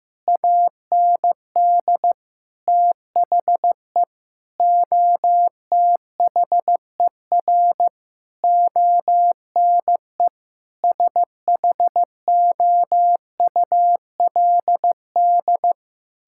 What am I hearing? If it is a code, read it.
Morse code, heard as a keyed tone, about 15 wpm, 700 Hz. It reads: AND THE OTHER ONE SHOULD
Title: Sentences from Top 1600 Words 15wpm